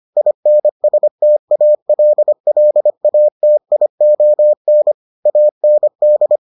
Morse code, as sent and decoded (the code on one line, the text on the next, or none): .. -. ... - .- .-.. .-.. .- - .. --- -. / .- -. -..
INSTALLATION AND